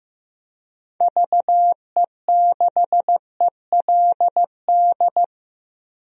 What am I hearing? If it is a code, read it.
Morse code, heard as a keyed tone, about 16 wpm, 700 Hz. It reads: VE6ELD
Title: DXCC-100 Call Signs - 6 Characters - Repeated 3x - 15wpm